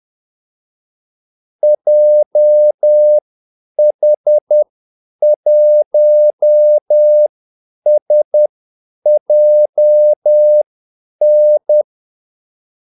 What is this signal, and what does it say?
Morse code, heard as a keyed tone, about 10 wpm, spaced as for 8 wpm, 600 Hz. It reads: JH1SJN
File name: Callsigns-10-8wpm